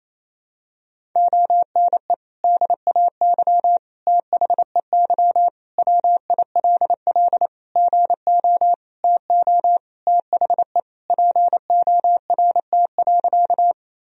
Morse code, heard as a keyed tone, about 28 wpm, 700 Hz. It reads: ONE DAY THEY WILL GO TO THE PORT.
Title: Sentences from Top 1000 Words - Review Part 1 - 28wpm